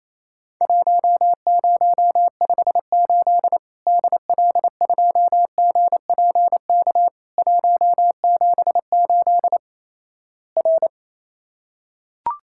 Morse code, heard as a keyed tone, about 28 wpm, 700 Hz. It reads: 1058 DL2GPK 178 R
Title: Worked All Europe DX Contest - QTC Exchange - 28wpm